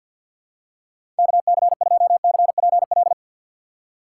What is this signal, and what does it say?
Morse code, heard as a keyed tone, about 52 wpm, 700 Hz. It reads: KC2CPL
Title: USA Call Signs - 6 Characters - 4x Character Spacing - 50wpm